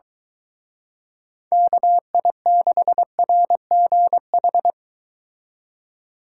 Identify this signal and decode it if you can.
Morse code, heard as a keyed tone, about 23 wpm, 700 Hz. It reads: KI6RGH